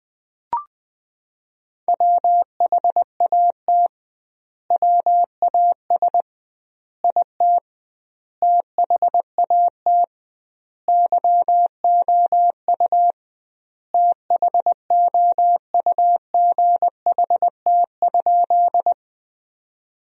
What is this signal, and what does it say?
Morse code, heard as a keyed tone, about 21 wpm, 700 Hz. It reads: WHAT WAS IT THAT YOU THOUGHT?
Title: 2x Word-Spacing - Sentences from Top 300 Words - 20wpm